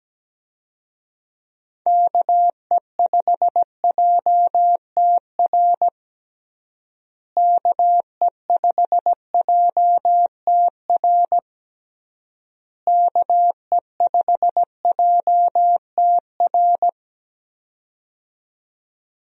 Morse code, heard as a keyed tone, about 17 wpm, 700 Hz. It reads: KE5JTR KE5JTR KE5JTR
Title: USA Call Signs - 6 Characters - Repeated 3x - 17wpm